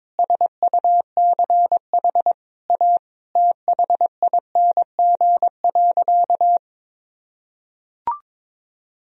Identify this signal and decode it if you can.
Morse code, heard as a keyed tone, about 22 wpm, 700 Hz. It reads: SUCH A THING.
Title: Sentences from Top 200 Words and Abbreviations 22wpm